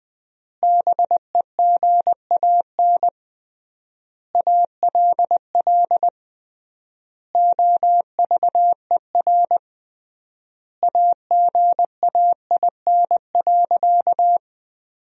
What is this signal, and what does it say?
Morse code, heard as a keyed tone, about 21 wpm, 700 Hz. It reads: BEGAN ALL OVER AGAIN.
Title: 3x Word-Spacing - Sentences from Top 500 Words - 20wpm